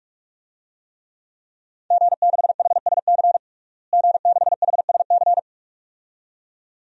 Morse code, heard as a keyed tone, about 47 wpm, 700 Hz. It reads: G6HSC G6HSC
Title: DXCC-100 Call Signs - 5 Characters - Repeated 2x - 45wpm